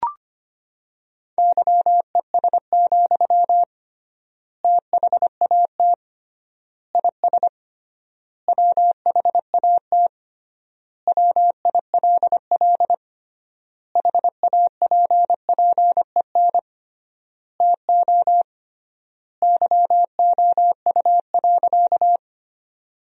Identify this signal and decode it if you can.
Morse code, heard as a keyed tone, about 25 wpm, 700 Hz. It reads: YES, THAT IS WHAT WILL HAPPEN TO YOU.
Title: 3x Word-Spacing - Sentences from Top 500 Words - 25wpm